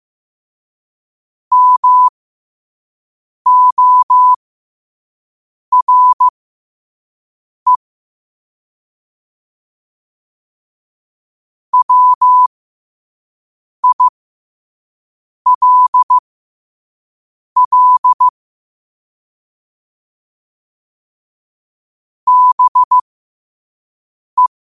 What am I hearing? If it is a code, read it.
Morse code, heard as a keyed tone, about 15 wpm, spaced as for 5 wpm, 1000 Hz. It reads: MORE WILL BE